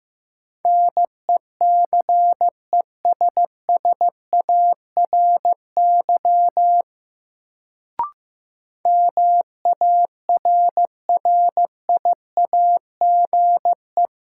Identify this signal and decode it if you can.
Morse code, heard as a keyed tone, about 15 wpm, 700 Hz. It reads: NECESSARY MARRIAGE
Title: Sentences from Top 800 Words - New Word Drill - 15wpm